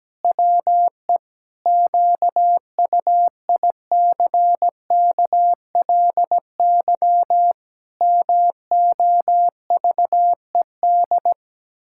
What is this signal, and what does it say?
Morse code, heard as a keyed tone, about 17 wpm, 700 Hz. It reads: WE QUICKLY MOVED
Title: Pangram Challenge v2 - 17wpm